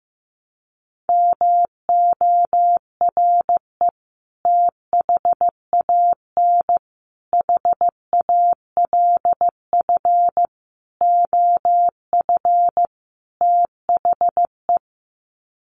Morse code, heard as a keyed tone, about 15 wpm, 700 Hz. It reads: MORE THAN HALF OF THE
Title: Sets of 5 Words 15wpm